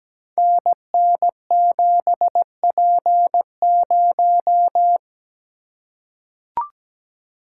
Morse code, heard as a keyed tone, about 17 wpm, 700 Hz. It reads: NN7P0
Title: Code Groups - 5 Letter:Numbers - 17wpm